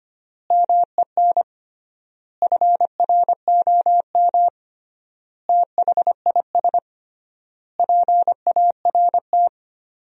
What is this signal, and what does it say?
Morse code, heard as a keyed tone, about 25 wpm, 700 Hz. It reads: MEN FROM THIS PART
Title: 3x Word-Spacing - Sentences from Top 300 Words - 25wpm